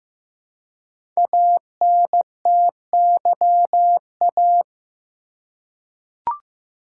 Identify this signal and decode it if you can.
Morse code, heard as a keyed tone, about 15 wpm, 700 Hz. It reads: ANTYA
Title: Code Groups - 5 Letter:Numbers - 15wpm